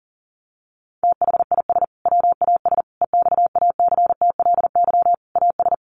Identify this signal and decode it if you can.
Morse code, heard as a keyed tone, about 40 wpm, 700 Hz. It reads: THIS WAS EXACTLY AS
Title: Sentences from Top 900 Words 40wpm